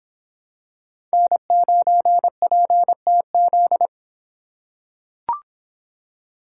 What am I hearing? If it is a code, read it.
Morse code, heard as a keyed tone, about 26 wpm, 700 Hz. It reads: N9PTZ